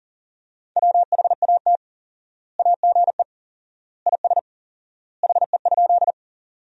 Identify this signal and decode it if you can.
Morse code, heard as a keyed tone, about 40 wpm, 700 Hz. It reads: WHAT AGE IS HE?
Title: 4x Word-Spacing - Sentences from Top 500 Words - 40wpm